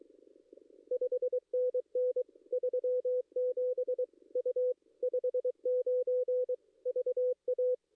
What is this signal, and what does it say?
Morse code, heard as a keyed tone, about 24 wpm, 470 Hz. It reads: 5NN 37 U 59 VA
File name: contests / ss-cw-k5tr-op-n5rz -2300